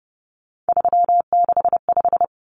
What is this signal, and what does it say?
Morse code, heard as a keyed tone, about 30 wpm, 700 Hz. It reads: 365
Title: Common Cut Numbers - 3-Digit 30wpm